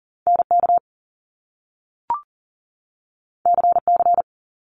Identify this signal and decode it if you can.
Morse code, heard as a keyed tone, about 42 wpm, 700 Hz. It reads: NK CC
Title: Top Two Letter Combinations in Words 40wpm